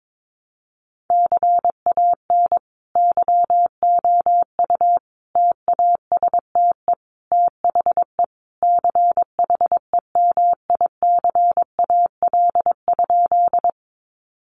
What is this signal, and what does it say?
Morse code, heard as a keyed tone, about 22 wpm, 700 Hz. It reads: CAN YOU TASTE THE CHEMICAL?
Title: Sentences from Top 1200 Words 22wpm